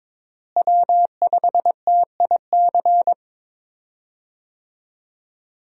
Morse code, heard as a keyed tone, about 22 wpm, 700 Hz. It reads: W5TIC
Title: USA Call Signs - 5 Characters - Repeated 2x - 22wpm